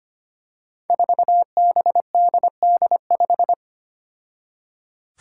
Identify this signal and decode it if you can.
Morse code, heard as a keyed tone, about 25 wpm, 700 Hz. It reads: 4BDD5
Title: Five Challenging Letter-Number Combinations 25wpm